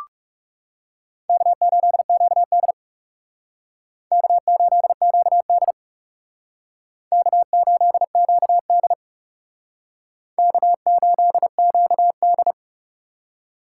Morse code, heard as a keyed tone, about 39 wpm, 700 Hz. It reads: K8QD K8QD K8QD K8QD